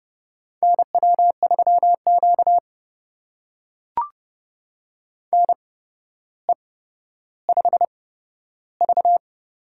Morse code, heard as a keyed tone, about 30 wpm, spaced as for 13 wpm, 700 Hz. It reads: NW3Q NE5V
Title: USA Call Signs - 4 Characters - 8x Character Spacing - 30wpm